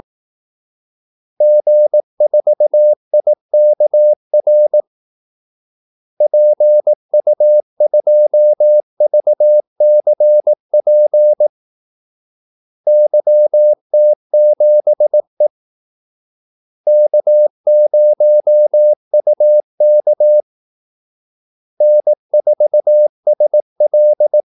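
Morse code, heard as a keyed tone, about 18 wpm, 600 Hz. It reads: G4IKR PU2VCP YT7E K0UK N4SL
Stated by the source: Morse call-sign practice at 18 wpm.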